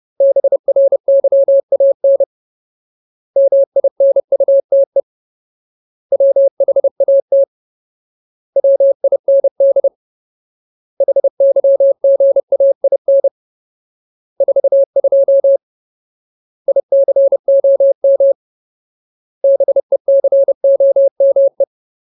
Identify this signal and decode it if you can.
Morse code, heard as a keyed tone, about 30 wpm, 550 Hz. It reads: BRYAN MINUTE WHAT WIND HYGAIN 42 ICOM BECOME
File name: Ultimate Word List / 30WPM Ultimate